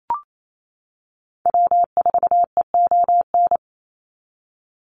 Morse code, heard as a keyed tone, about 28 wpm, 700 Hz. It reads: W4EON
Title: NA SOTA Activator Callsigns 28wpm